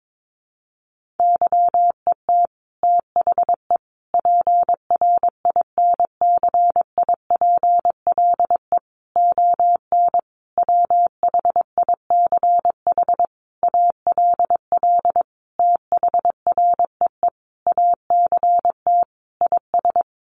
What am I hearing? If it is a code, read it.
Morse code, heard as a keyed tone, about 22 wpm, 700 Hz. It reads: YET THE PRINCIPLE ON WHICH ALL THREE ACT IS
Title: Sentences from Top 1400 Words 22wpm